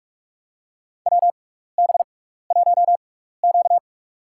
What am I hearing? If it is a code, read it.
Morse code, heard as a keyed tone, about 42 wpm, spaced as for 15 wpm, 700 Hz. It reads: WB1Q
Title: USA Call Signs - 6x Character Spacing - 45wpm